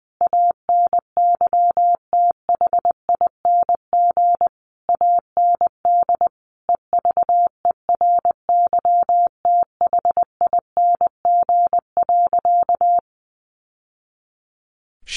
Morse code, heard as a keyed tone, about 20 wpm, 700 Hz. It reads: ANYTHING AND EVERYTHING.
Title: Sentences from Top 1200 Words 20wpm